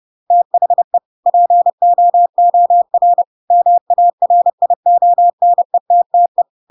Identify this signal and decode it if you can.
Morse code, heard as a keyed tone, about 30 wpm, 700 Hz. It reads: THE POOR MARIONETTE
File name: Pinocchio / Pinocchio ch5 30 30 ik1hge